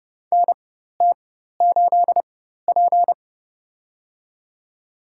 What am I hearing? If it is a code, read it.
Morse code, heard as a keyed tone, about 30 wpm, spaced as for 14 wpm, 700 Hz. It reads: NT8P
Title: USA Call Signs - 4 Characters - 4x Character Spacing - 30wpm